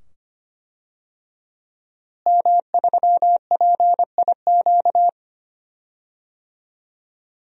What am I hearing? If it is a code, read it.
Morse code, heard as a keyed tone, about 25 wpm, 700 Hz. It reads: M3PIQ
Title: DXCC-100 Call Signs - 5 Characters - Repeated 2x - 25wpm